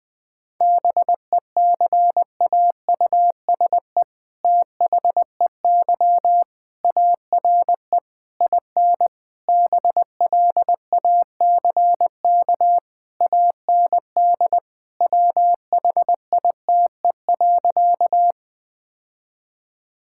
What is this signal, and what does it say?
Morse code, heard as a keyed tone, about 21 wpm, 700 Hz. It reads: BECAUSE THEY ARE IN BLACK AND WHITE.
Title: All-Age Appropriate Jokes - 20wpm